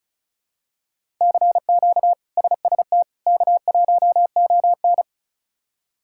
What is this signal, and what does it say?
Morse code, heard as a keyed tone, about 34 wpm, 700 Hz. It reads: CQ SST K1ON
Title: K1USN SST QSO Practice - 35wpm